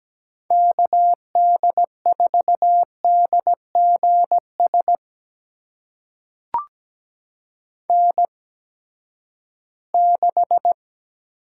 Morse code, heard as a keyed tone, about 17 wpm, spaced as for 12 wpm, 700 Hz. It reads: KD4DGS N6
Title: USA Call Signs - 8x Character Spacing - 17wpm